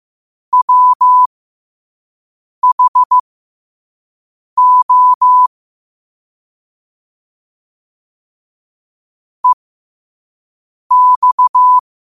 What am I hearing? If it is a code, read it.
Morse code, heard as a keyed tone, about 15 wpm, spaced as for 5 wpm, 1000 Hz. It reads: WHO EX
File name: qotdfiles / 5wpm